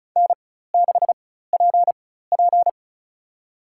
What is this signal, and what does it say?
Morse code, heard as a keyed tone, about 36 wpm, spaced as for 16 wpm, 700 Hz. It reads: N6PP
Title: USA Call Signs - 4x Character Spacing - 35wpm